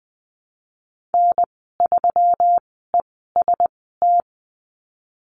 Morse code, heard as a keyed tone, about 20 wpm, spaced as for 15 wpm, 700 Hz. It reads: N3EST
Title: 147 Lesson 24g - Letter y - Call Signs 20wpm